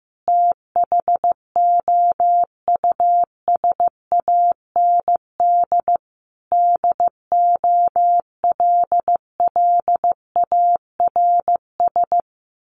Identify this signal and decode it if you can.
Morse code, heard as a keyed tone, about 15 wpm, 700 Hz. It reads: THOUSAND DOLLARS
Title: A Sound Of Thunder 15wpm